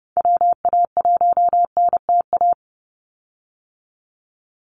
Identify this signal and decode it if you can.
Morse code, heard as a keyed tone, about 30 wpm, 700 Hz. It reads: WA1NTA
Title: Call Signs Repeated 2x - 30wpm